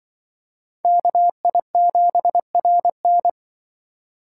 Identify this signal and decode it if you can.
Morse code, heard as a keyed tone, about 25 wpm, 700 Hz. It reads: KI7RN
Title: Speed Racing - USA Call Signs 20wpm